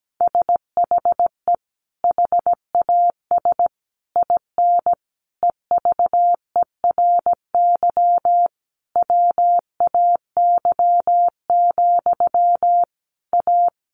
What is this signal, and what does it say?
Morse code, heard as a keyed tone, about 17 wpm, 700 Hz. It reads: SHE HAS IN EVERY WAY, A
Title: Sentences from Top 1300 Words 17wpm